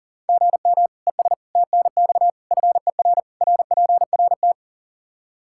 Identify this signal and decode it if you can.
Morse code, heard as a keyed tone, about 40 wpm, 700 Hz. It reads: GM ES TNX FER RPRT
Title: K1USN SST QSO Practice 40wpm